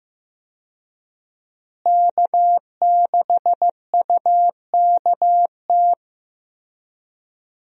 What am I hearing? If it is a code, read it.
Morse code, heard as a keyed tone, about 15 wpm, 700 Hz. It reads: K6UKT
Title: USA Call Signs - 5 Characters - Repeated 3x - 15wpm